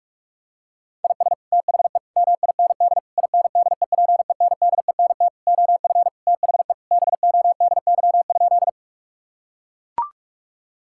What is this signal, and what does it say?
Morse code, heard as a keyed tone, about 41 wpm, 700 Hz. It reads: IS THE MIND INDEPENDENT OF THE BODY?
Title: Sentences from Top 1000 Words - Review Part 6 - 45wpm